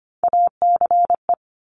Morse code, heard as a keyed tone, about 25 wpm, 700 Hz. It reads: ACE
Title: Top Three Letter Combinations in Words 25wpm